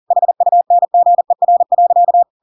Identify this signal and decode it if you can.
Morse code, heard as a keyed tone, about 40 wpm, 700 Hz. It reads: HUNGER.
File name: Pinocchio / Pinocchio ch5 40 40 ik1hge